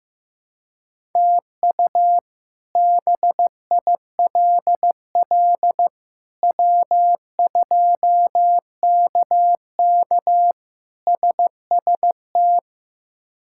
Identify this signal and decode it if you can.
Morse code, heard as a keyed tone, about 15 wpm, 700 Hz. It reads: TU BILL W2KK SST
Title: K1USN SST QSO Practice - 15wpm